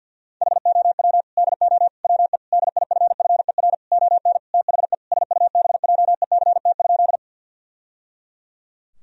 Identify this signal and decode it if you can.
Morse code, heard as a keyed tone, about 50 wpm, 700 Hz. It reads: HOW DO WE DIFFER ON THE SUBJECT?
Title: Sentences from Top 1000 Words - Review Part 1 - 50wpm